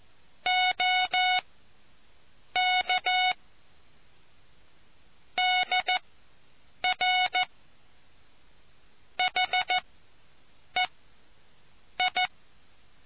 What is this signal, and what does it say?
Morse code, heard as a keyed tone, about 14 wpm, spaced as for 7 wpm, 740 Hz. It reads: OK DR HEI